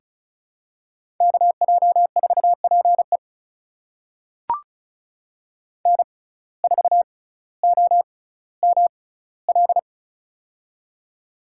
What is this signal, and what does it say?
Morse code, heard as a keyed tone, about 36 wpm, spaced as for 18 wpm, 700 Hz. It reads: KJ4PE N4OML
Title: USA Call Signs - 5 Characters - 6x Character Spacing - 35wpm